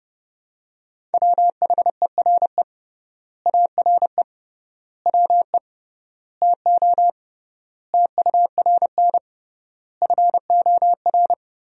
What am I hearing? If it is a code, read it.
Morse code, heard as a keyed tone, about 30 wpm, 700 Hz. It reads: WHERE ARE WE TO TURN FOR
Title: 3x Word-Spacing - Sentences from Top 200 Words - 30wpm